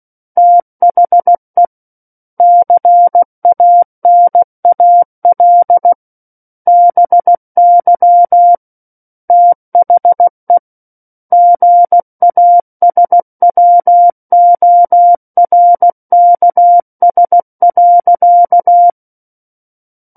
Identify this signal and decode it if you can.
Morse code, heard as a keyed tone, about 16 wpm, 700 Hz. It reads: THE CANAL BY THE GASWORKS.
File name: Book One / 07-Chapter 7 - 16 WPM